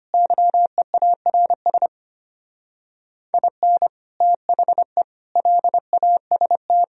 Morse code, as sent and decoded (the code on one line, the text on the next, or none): -.-- . .- .-. ... / .. -. / - .... / .-.. .- ... -
YEARS IN THE LAST